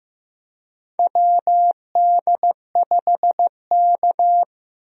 Morse code, heard as a keyed tone, about 15 wpm, 700 Hz. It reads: WD5K
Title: USA Call Signs - 4 Characters - Repeated 3x - 15wpm